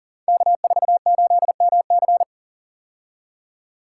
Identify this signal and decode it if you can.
Morse code, heard as a keyed tone, about 37 wpm, 700 Hz. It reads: K48MC
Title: Code Groups - 5 Letter:Numbers - 40wpm